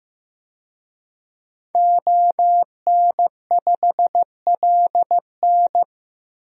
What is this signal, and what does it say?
Morse code, heard as a keyed tone, about 15 wpm, 700 Hz. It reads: ON5LN
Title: DXCC-100 Call Signs - 5 Characters - Repeated 2x - 15wpm